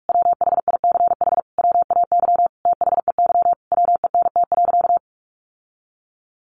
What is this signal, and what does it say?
Morse code, heard as a keyed tone, about 44 wpm, 700 Hz. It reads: WHICH WAY THEY WENT.
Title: Sentences from Top 1300 Words 45wpm